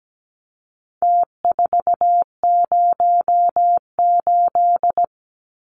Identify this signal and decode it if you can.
Morse code, heard as a keyed tone, about 17 wpm, 700 Hz. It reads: T408
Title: Numbers - Contest Style - 4-Digit 17wpm